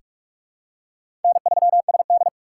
Morse code, heard as a keyed tone, about 46 wpm, 700 Hz. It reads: N3SD
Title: USA Call Signs - 2x Character Spacing - 45wpm